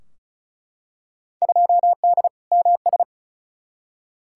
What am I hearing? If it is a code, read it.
Morse code, heard as a keyed tone, about 34 wpm, 700 Hz. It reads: 2D MS
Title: Field Day Exchanges Encore - 35wpm